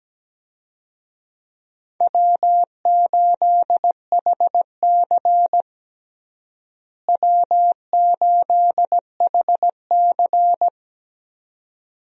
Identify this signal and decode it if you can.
Morse code, heard as a keyed tone, about 17 wpm, 700 Hz. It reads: W8HC W8HC
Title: USA Call Signs - 4 Characters - Repeated 2x - 17wpm